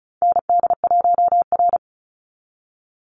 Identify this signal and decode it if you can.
Morse code, heard as a keyed tone, about 34 wpm, 700 Hz. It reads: ND1R
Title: Call Signs Repeated 3x - 35wpm